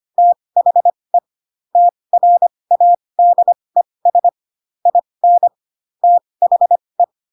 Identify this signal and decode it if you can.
Morse code, heard as a keyed tone, about 25 wpm, spaced as for 20 wpm, 700 Hz. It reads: THE TRADES IN THE
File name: Pinocchio / Pinocchio ch4 15 25 ik1hge